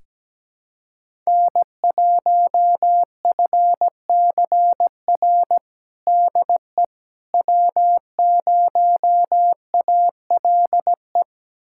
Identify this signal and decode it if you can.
Morse code, heard as a keyed tone, about 17 wpm, 700 Hz. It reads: N1FCR DE W0ALE
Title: USA Callsign DE Callsign - 5 Characters - 17wpm